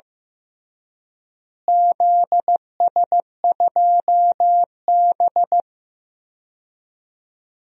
Morse code, heard as a keyed tone, about 15 wpm, 700 Hz. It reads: ZS2B
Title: DXCC-100 Call Signs - 4 Characters - Repeated 3x - 15wpm